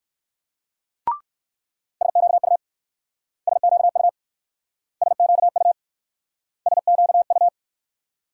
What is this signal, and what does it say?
Morse code, heard as a keyed tone, about 62 wpm, 690 Hz. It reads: SQU SQU SQU SQU